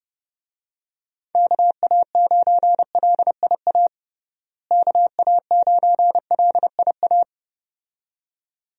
Morse code, heard as a keyed tone, about 30 wpm, 700 Hz. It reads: KA9LIA KA9LIA
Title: USA Call Signs - 6 Characters - Repeated 2x - 30wpm